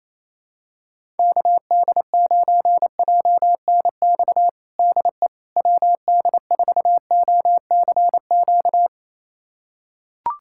Morse code, heard as a keyed tone, about 28 wpm, 700 Hz. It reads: KD9JNX DE WD4OCQ
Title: USA Callsign DE Callsign 28wpm